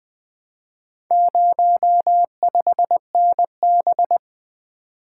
Morse code, heard as a keyed tone, about 20 wpm, 700 Hz. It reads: 05NB
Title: Four Letter-Number Combinations - 20wpm